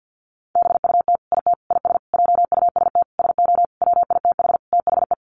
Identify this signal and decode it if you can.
Morse code, heard as a keyed tone, about 50 wpm, 700 Hz. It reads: BUT IT IS JUST SO WITH THE
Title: Sentences from Top 200 Words 50wpm